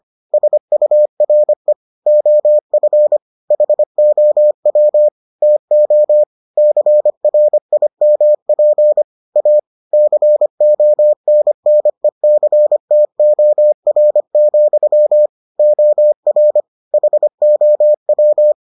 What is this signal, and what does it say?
Morse code, heard as a keyed tone, about 25 wpm, 600 Hz. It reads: SURE OF HOW TO CRIMP A CONNECTOR, OR HOW